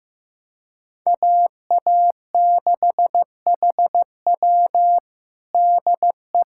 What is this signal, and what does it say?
Morse code, heard as a keyed tone, about 15 wpm, 700 Hz. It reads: AA6HW DE
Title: USA Callsign DE Callsign 15wpm